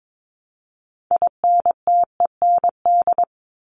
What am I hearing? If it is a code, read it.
Morse code, heard as a keyed tone, about 22 wpm, 700 Hz. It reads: INTEND
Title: Top 1501-1750 Words 22wpm